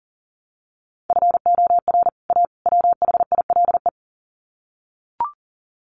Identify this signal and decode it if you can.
Morse code, heard as a keyed tone, about 40 wpm, 700 Hz. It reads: FOR A WHILE
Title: Sets of 3 Words 40wpm